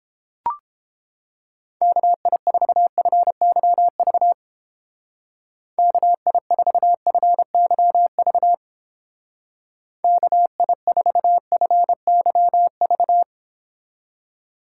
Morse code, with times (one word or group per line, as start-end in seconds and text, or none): 1.8-4.3 KI4FYV
5.8-8.5 KI4FYV
10.0-13.2 KI4FYV